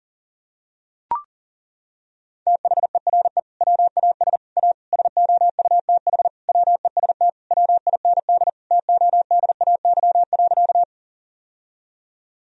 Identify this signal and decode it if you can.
Morse code, heard as a keyed tone, about 37 wpm, 700 Hz. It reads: THERE WAS A SOUTH WEST WIND TODAY.
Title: Sentences from Top 1000 Words - Review Part 2 - 40wpm